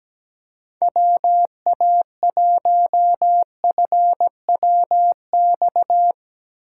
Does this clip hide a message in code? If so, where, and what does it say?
Yes: Morse code from 0.8 to 6.1 s, heard as a keyed tone, about 22 wpm, 700 Hz. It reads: WA1FWX